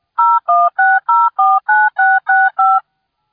DTMF, heard as telephone touch tones, tones of about 220 ms. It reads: 016049665